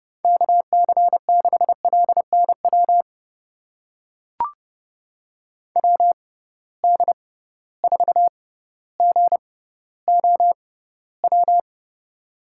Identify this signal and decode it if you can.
Morse code, heard as a keyed tone, about 30 wpm, spaced as for 15 wpm, 700 Hz. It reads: KC6LNW WD4GOW
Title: USA Call Signs - 6 Characters - 6x Character Spacing - 30wpm